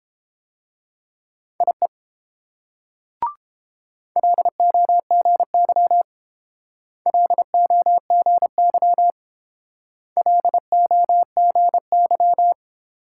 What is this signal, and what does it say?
Morse code, heard as a keyed tone, about 28 wpm, 700 Hz. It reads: IE LOGY LOGY LOGY
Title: Speed-Racing Multi-letter Phonemes - 22wpm